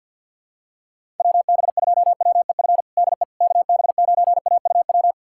Morse code, heard as a keyed tone, about 49 wpm, 700 Hz. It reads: WB2WEF DE KB9AUW
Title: USA Callsign DE Callsign 50wpm